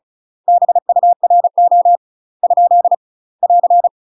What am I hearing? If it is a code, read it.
Morse code, heard as a keyed tone, about 34 wpm, 700 Hz. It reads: BURO ? <AR>
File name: UK MASTER QSOs TRAINING / 35WPM / UK MASTER QSOs 0001